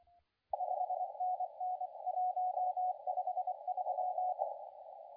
Morse code, heard as a keyed tone, about 24 wpm, 720 Hz, with radio noise. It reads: 5NN 153
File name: la4c / sac-cw-2017 / CONTEST AUDIO 2100Z